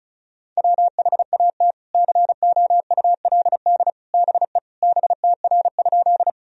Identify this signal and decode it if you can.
Morse code, heard as a keyed tone, about 34 wpm, 700 Hz. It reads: WHAT COULD BE BTR?
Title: Sentences from Top 400 Words and Abbreviations 35wpm